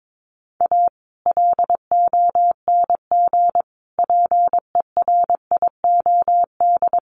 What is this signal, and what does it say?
Morse code, heard as a keyed tone, about 22 wpm, 700 Hz. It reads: A LONG PERIOD
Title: Sets of 5 Words 22wpm